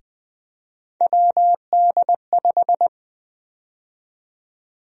Morse code, heard as a keyed tone, about 20 wpm, 700 Hz. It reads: WD5
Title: USA Call Sign Prefixes - Repeated 2x - 20wpm